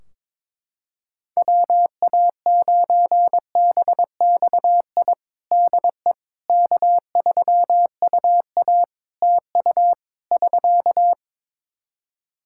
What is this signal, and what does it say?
Morse code, heard as a keyed tone, about 22 wpm, 700 Hz. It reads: WA9BXI DE K3UA TU <SK>
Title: K1USN SST QSO Practice 22wpm